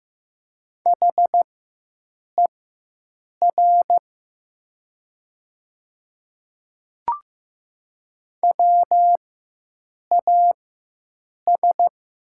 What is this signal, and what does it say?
Morse code, heard as a keyed tone, about 15 wpm, spaced as for 7 wpm, 700 Hz. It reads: HER WAS
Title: 4x Character Spacing - Top 100 Words - 15wpm